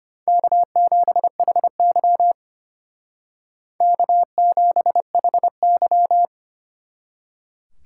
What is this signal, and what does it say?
Morse code, heard as a keyed tone, about 30 wpm, 700 Hz. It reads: K7HY K7HY
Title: Speed Racing - USA 1x2 Call Signs 25wpm